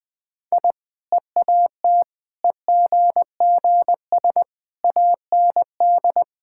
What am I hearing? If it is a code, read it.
Morse code, heard as a keyed tone, about 21 wpm, 700 Hz. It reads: I EAT EGGS AND
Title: Easy Sentences with Hints - 20wpm